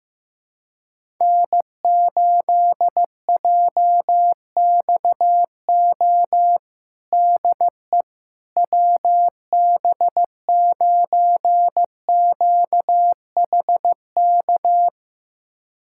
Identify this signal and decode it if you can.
Morse code, heard as a keyed tone, about 15 wpm, 700 Hz. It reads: N8JXO DE WB9QHK
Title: USA Callsign DE Callsign 15wpm